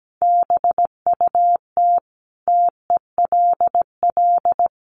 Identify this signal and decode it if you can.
Morse code, heard as a keyed tone, about 17 wpm, 700 Hz. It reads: BUT TELL
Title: Sentences from Top 1500 Words 17wpm